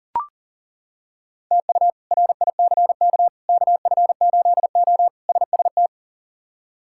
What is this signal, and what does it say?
Morse code, heard as a keyed tone, about 40 wpm, 700 Hz. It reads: TU RICK KF8O SST
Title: K1USN SST QSO Practice - 40wpm